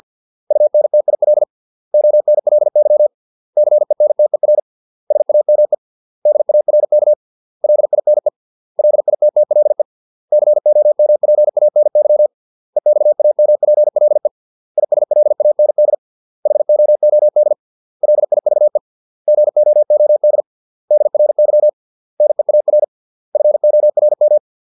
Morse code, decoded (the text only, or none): UNTIL ONLY CENTER SAME DARK LINE LITTLE COMPANY EXAMPLE ISLAND HOOD LIVE GOOD DRY NEAR FORM